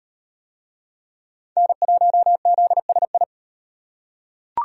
Morse code, heard as a keyed tone, about 33 wpm, 700 Hz. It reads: N1ZSI